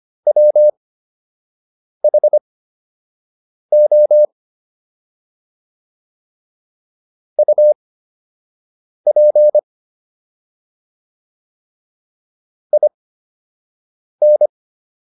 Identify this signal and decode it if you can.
Morse code, heard as a keyed tone, about 25 wpm, spaced as for 6 wpm, 600 Hz. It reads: WHO UP IN